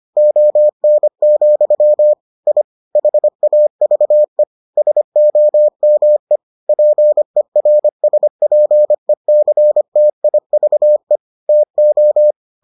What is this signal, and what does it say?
Morse code, heard as a keyed tone, about 25 wpm, 600 Hz. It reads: ON, I HAVE SOME PERSPECTIVE TO